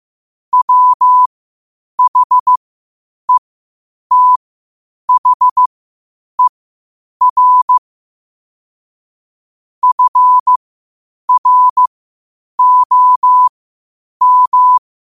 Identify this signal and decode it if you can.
Morse code, heard as a keyed tone, about 15 wpm, spaced as for 8 wpm, 1000 Hz. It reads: WHETHER FROM
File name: qotdfiles / 8wpm